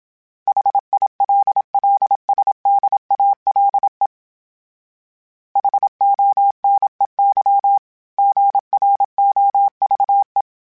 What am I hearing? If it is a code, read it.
Morse code, heard as a keyed tone, about 27 wpm, 800 Hz. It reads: HILLSDALE HONEY GROVE
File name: fromSMs / FromSMs mp3s / Cities@25